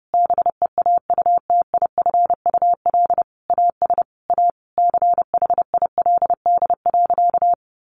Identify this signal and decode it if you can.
Morse code, heard as a keyed tone, about 28 wpm, 700 Hz. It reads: BEAUTIFUL AS A CHILD.
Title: Sentences from Top 1500 Words 30wpm